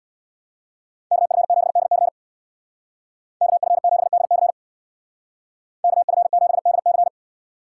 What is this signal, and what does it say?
Morse code, heard as a keyed tone, about 61 wpm, 700 Hz. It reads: KF7DC KF7DC KF7DC
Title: Speed Racing - USA 2x2 Call Signs 50wpm